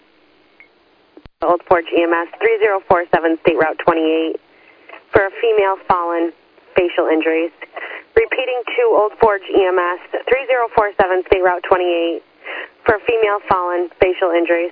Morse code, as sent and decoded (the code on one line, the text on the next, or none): none